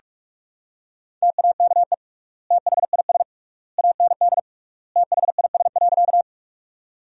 Morse code, heard as a keyed tone, about 41 wpm, 700 Hz. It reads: TAKE THIS AND THIS.
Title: 3x Word-Spacing - Sentences from Top 200 Words - 45wpm